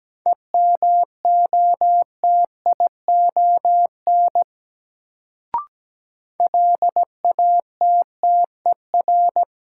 Morse code, heard as a keyed tone, about 17 wpm, 700 Hz. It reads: EMOTION LATTER
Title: Sentences from Top 1600 Words - New Word Drill - 17wpm